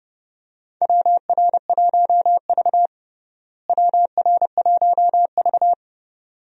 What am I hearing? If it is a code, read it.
Morse code, heard as a keyed tone, about 31 wpm, 700 Hz. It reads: WR1V WR1V
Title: USA Call Signs - 4 Characters - Repeated 2x - 30wpm